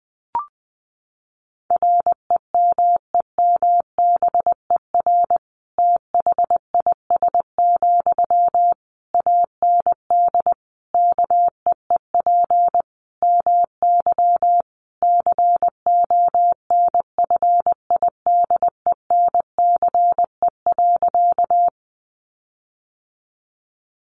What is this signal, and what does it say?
Morse code, heard as a keyed tone, about 20 wpm, 700 Hz. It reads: REMEMBER THIS, AND KEEP MY CONFIDENCE.
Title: Sentences from Top 1500 Words 20wpm